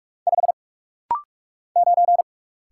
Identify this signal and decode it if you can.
Morse code, heard as a keyed tone, about 46 wpm, 700 Hz. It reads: 5 9
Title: Single Letter-Number - Rapid-Fire 45wpm